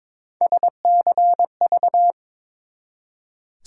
Morse code, heard as a keyed tone, about 22 wpm, 700 Hz. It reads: SCV
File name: ARRL Sections v2 - 22wpm